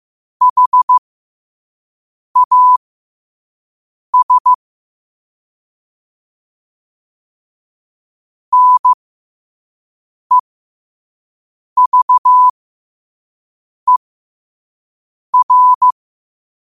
Morse code, heard as a keyed tone, about 15 wpm, spaced as for 5 wpm, 1000 Hz. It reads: HAS NEVER
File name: qotdfiles / 5wpm